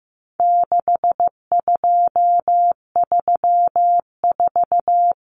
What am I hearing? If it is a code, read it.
Morse code, heard as a keyed tone, about 15 wpm, 700 Hz. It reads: 6234
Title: Common Cut Numbers - 4-Digit 15wpm